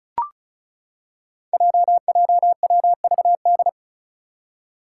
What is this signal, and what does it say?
Morse code, heard as a keyed tone, about 36 wpm, 700 Hz. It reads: JJWVD
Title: Code Groups - 5 Letters - 35wpm